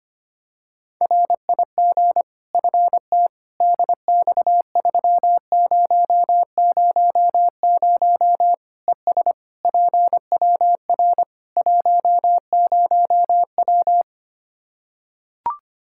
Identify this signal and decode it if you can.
Morse code, heard as a keyed tone, about 25 wpm, 700 Hz. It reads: RIG FT DX3000 ES PWR 10W
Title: Practice QSO - Exchange 2 - Info - 25wpm